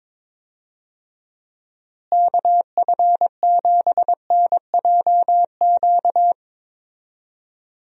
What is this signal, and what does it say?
Morse code, heard as a keyed tone, about 22 wpm, 700 Hz. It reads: KF7NJQ
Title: USA Call Signs - 6 Characters - Repeated 3x - 22wpm